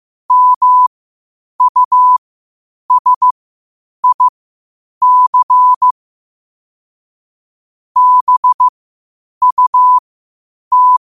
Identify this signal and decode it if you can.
Morse code, heard as a keyed tone, about 15 wpm, spaced as for 8 wpm, 1000 Hz. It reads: MUSIC BUT